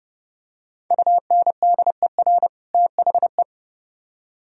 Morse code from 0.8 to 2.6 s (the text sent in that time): UNDER